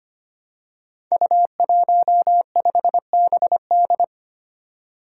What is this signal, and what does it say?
Morse code, heard as a keyed tone, about 25 wpm, 700 Hz. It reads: U15BD
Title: Five Challenging Letter-Number Combinations 25wpm